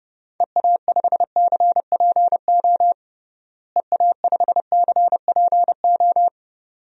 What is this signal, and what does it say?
Morse code, heard as a keyed tone, about 30 wpm, 700 Hz. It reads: EA5CPO EA5CPO
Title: DXCC-100 Call Signs - 6 Characters - Repeated 2x - 30wpm